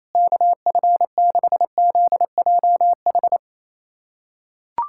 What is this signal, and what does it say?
Morse code, heard as a keyed tone, about 28 wpm, 700 Hz. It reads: KF6ZJH
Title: USA Callsign DE Callsign 28wpm